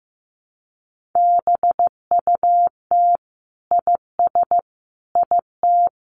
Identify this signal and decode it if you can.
Morse code, heard as a keyed tone, about 15 wpm, 700 Hz. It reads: BUT IS IT
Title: Sentences from Top 1500 Words 15wpm